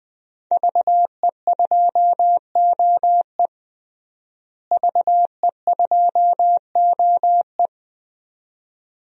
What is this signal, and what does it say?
Morse code, heard as a keyed tone, about 20 wpm, 700 Hz. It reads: VE2OE VE2OE
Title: DXCC-100 Call Signs - 5 Characters - Repeated 2x - 20wpm